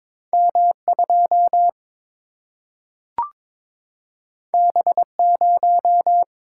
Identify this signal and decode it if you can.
Morse code, heard as a keyed tone, about 22 wpm, 700 Hz. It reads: M2 B0
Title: Single-Letter Single-Number 22wpm